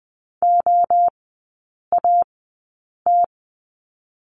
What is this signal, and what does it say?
Morse code, heard as a keyed tone, about 20 wpm, 700 Hz. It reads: O A T
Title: 029 Lesson 6e - Letter i - Three Characters 20wpm